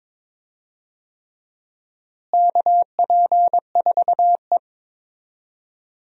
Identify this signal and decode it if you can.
Morse code, heard as a keyed tone, about 22 wpm, 700 Hz. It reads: KP4E
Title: DXCC-100 Call Signs - 4 Characters - Repeated 3x - 22wpm